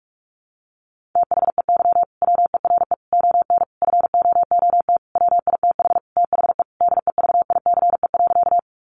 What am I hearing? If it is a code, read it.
Morse code, heard as a keyed tone, about 44 wpm, 700 Hz. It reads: THEY WERE ON FOOT WITH THE DEVICE.
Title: Sentences from Top 1500 Words 45wpm